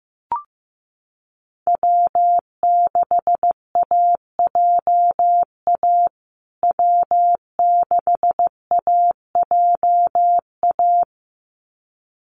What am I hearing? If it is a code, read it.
Morse code, heard as a keyed tone, about 15 wpm, 700 Hz. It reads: W6AJA W6AJA
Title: USA Call Signs Repeated 2x 15wpm